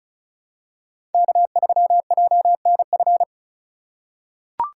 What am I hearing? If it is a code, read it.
Morse code, heard as a keyed tone, about 34 wpm, 700 Hz. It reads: K3JNF
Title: USA Call Signs - 4x Character Spacing - 35wpm